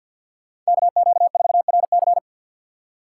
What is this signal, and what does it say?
Morse code, heard as a keyed tone, about 51 wpm, 700 Hz. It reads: KQ4RC
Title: USA Call Signs - 5 Characters - 3x Character Spacing - 50wpm